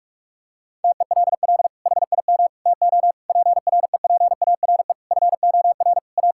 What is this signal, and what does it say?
Morse code, heard as a keyed tone, about 46 wpm, 700 Hz. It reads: TELL HIM TO PREPARE FOR A
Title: Sentences from Top 1000 Words - Review Part 6 - 45wpm